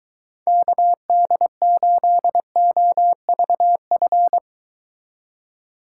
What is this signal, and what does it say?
Morse code, heard as a keyed tone, about 23 wpm, 700 Hz. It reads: KD8OVF